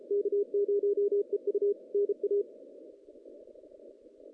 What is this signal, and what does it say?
Morse code, heard as a keyed tone, about 32 wpm, 380 Hz. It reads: K0EU NA
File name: 2016 JAN CW NAQP / 2016 Jan naqp k5ot - 0-11